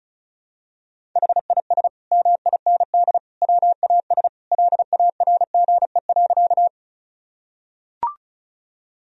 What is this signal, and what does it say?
Morse code, heard as a keyed tone, about 34 wpm, 700 Hz. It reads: HIS MIND WAS LARGE.
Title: Sentences from Top 1000 Words - Review Part 1 - 35wpm